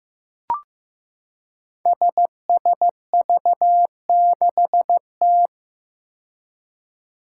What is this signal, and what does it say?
Morse code, heard as a keyed tone, about 15 wpm, 700 Hz. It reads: SSV6T
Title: Five Challenging Letter-Number Combinations 15wpm